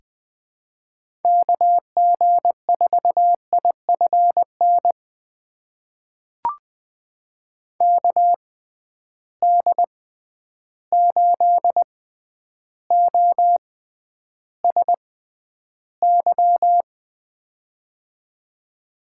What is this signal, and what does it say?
Morse code, heard as a keyed tone, about 20 wpm, spaced as for 10 wpm, 700 Hz. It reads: KG4IFN KD8OSY
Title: USA Call Signs - 6 Characters - 6x Character Spacing - 20wpm